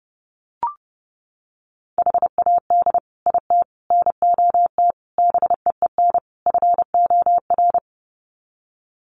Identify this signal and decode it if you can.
Morse code, heard as a keyed tone, about 30 wpm, 700 Hz. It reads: HAD IT NOT BEEN FOR
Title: Sets of 5 Words Encore2 30wpm